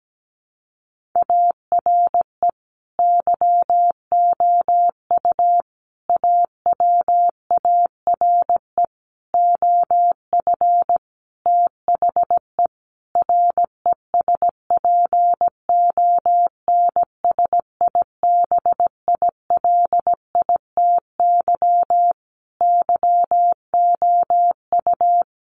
Morse code, heard as a keyed tone, about 17 wpm, 700 Hz. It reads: ARE YOU AWARE OF THE RESPONSIBILITY YOU
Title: Sentences from Top 1500 Words 17wpm